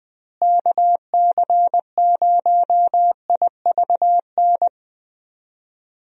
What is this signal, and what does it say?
Morse code, heard as a keyed tone, about 20 wpm, 700 Hz. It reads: KC0IVN
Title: USA Call Signs - 6 Characters - Repeated 2x - 20wpm